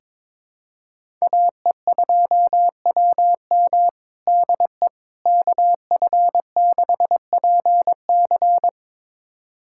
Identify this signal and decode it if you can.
Morse code, heard as a keyed tone, about 22 wpm, 700 Hz. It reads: AE2WM DE KF6PC
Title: USA Callsign DE Callsign - 5 Characters - 22wpm